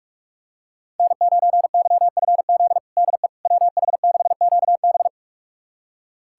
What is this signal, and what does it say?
Morse code, heard as a keyed tone, about 46 wpm, 700 Hz. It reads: N9YFZ DE WH6QB
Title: USA Callsign DE Callsign - 5 Characters - 45wpm